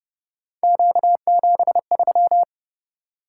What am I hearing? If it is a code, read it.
Morse code, heard as a keyed tone, about 29 wpm, 700 Hz. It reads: Q73
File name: Speed-Racing - Three Letter-Number Combinations - 25wpm